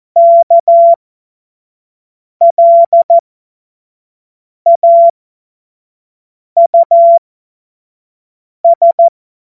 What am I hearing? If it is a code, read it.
Morse code, heard as a keyed tone, about 14 wpm, spaced as for 5 wpm, 680 Hz. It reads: KLAUS
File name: Morsekurs / Witze / Witz 28. Dezember 2021, 25 F70